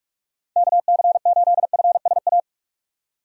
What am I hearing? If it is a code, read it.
Morse code, heard as a keyed tone, about 46 wpm, 700 Hz. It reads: KC8FSA
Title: USA Call Signs - 6 Characters - 45wpm